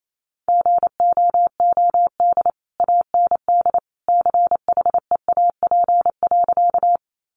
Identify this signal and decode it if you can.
Morse code, heard as a keyed tone, about 28 wpm, 700 Hz. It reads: GOOD AND CHEAP.
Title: Sentences from Top 1200 Words 28wpm